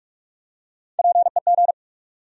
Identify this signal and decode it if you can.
Morse code, heard as a keyed tone, about 46 wpm, 700 Hz. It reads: PEG
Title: USA Common Names - Max 5 Letters - 45wpm